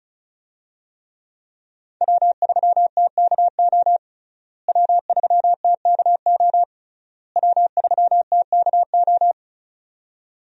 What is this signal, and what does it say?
Morse code, heard as a keyed tone, about 36 wpm, 700 Hz. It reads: W3TKO W3TKO W3TKO
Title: USA Call Signs - 5 Characters - Repeated 3x - 35wpm